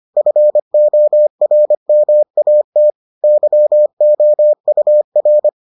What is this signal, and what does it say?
Morse code, heard as a keyed tone, about 25 wpm, 600 Hz. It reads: FORMAT YOUR